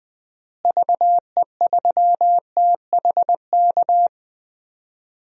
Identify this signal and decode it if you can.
Morse code, heard as a keyed tone, about 20 wpm, 700 Hz. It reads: VE3THK
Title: DXCC-100 Call Signs - 6 Characters - Repeated 3x - 20wpm